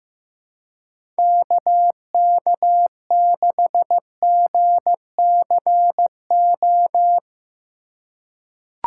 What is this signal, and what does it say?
Morse code, heard as a keyed tone, about 16 wpm, 700 Hz. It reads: KK6GCO
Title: USA Call Signs - 6 Characters - 2x Character Spacing - 15wpm